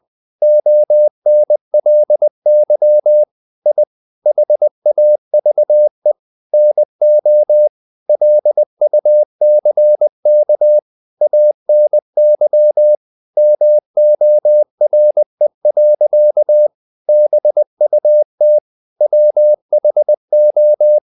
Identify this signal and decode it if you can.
Morse code, heard as a keyed tone, about 20 wpm, 600 Hz. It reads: ONLY I HAVE NO LUCK ANY MORE. BUT WHO
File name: the old man and the sea / the old man and the sea part 1 20wpm